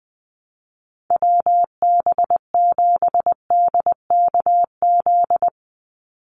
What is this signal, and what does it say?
Morse code, heard as a keyed tone, about 20 wpm, 700 Hz. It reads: WB7DKZ
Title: Call Signs - 20wpm